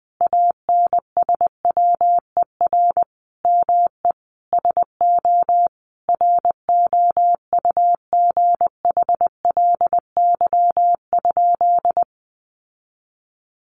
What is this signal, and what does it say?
Morse code, heard as a keyed tone, about 20 wpm, 700 Hz. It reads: ANSWER ME SO ROUGHLY?
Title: Sentences from Top 1500 Words 20wpm